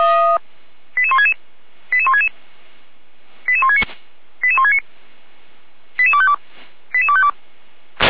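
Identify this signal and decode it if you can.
Five-tone selective call (ZVEI), heard as quick runs of five tones, tones of about 70 ms. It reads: 88166 88166 88169 88169 88252 88252
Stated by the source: ZVEI five-tone selective call 88253, received by SDR.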